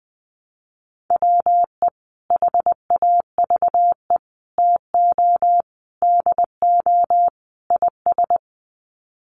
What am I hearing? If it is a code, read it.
Morse code, heard as a keyed tone, about 20 wpm, 700 Hz. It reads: WE HAVE TO DO IS
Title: Sets of 5 Words Encore 20wpm